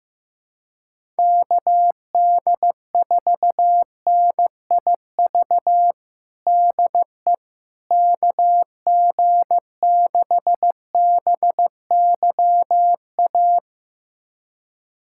Morse code, heard as a keyed tone, about 15 wpm, 700 Hz. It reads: KD4NIV DE KG6BYA
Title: USA Callsign DE Callsign 15wpm